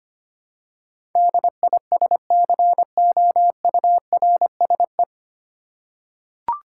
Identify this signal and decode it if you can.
Morse code, heard as a keyed tone, about 25 wpm, 700 Hz. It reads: DISCOURSE
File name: practice / 25wpm / 9 Letter Words - First 3 Letters Given - 25wpm